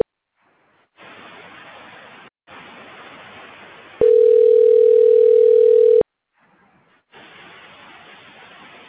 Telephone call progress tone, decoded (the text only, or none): ringback tone